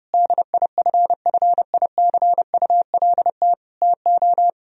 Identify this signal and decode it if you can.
Morse code, heard as a keyed tone, about 30 wpm, 700 Hz. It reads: DIFFICULT TO
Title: Sentences from Top 1000 Words - Review Part 4 - 30wpm